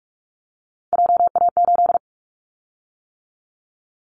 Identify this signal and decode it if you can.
Morse code, heard as a keyed tone, about 44 wpm, 700 Hz. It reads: JA8
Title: Callsign Prefixes for Top 25 DXCC - 45wpm